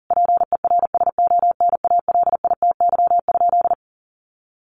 Morse code, heard as a keyed tone, about 40 wpm, 700 Hz. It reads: PERSONALITY?
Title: Sentences from Top 1400 Words 40wpm